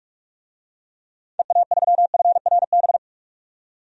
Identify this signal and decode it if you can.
Morse code, heard as a keyed tone, about 46 wpm, 700 Hz. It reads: EA3FRB
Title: DXCC-100 Call Signs - 6 Characters - 45wpm